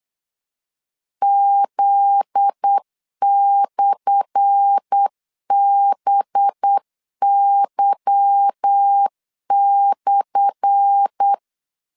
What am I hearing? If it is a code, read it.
Morse code, heard as a keyed tone, about 8 wpm, 800 Hz. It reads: Z/BY/
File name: LEZIONI / 10 SIGNS / 10A SLASH 8wpm 3452